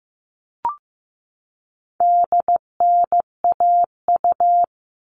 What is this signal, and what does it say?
Morse code, heard as a keyed tone, about 15 wpm, 700 Hz. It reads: DNAU
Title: All Cut Numbers - 4-Digit 15wpm